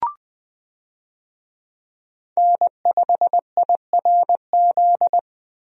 Morse code, heard as a keyed tone, about 20 wpm, 700 Hz. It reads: N5IRZ
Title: USA Call Signs - 5 Characters - 20wpm